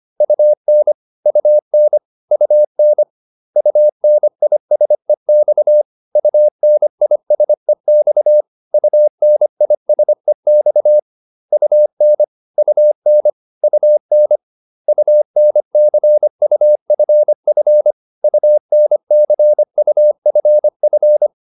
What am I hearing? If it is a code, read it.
Morse code, heard as a keyed tone, about 25 wpm, 600 Hz. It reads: UN UN UN UNISEX UNISEX UNISEX UN UN UN UNCUFF UNCUFF